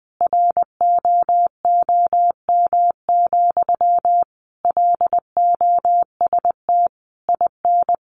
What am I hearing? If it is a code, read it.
Morse code, heard as a keyed tone, about 20 wpm, 700 Hz. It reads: ROOM, LOST IN
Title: Sentences from Top 1400 Words 20wpm